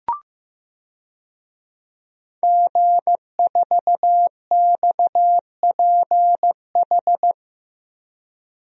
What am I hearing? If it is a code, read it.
Morse code, heard as a keyed tone, about 15 wpm, 700 Hz. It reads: G4XPH
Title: DXCC-100 Call Signs - 5 Characters - 15wpm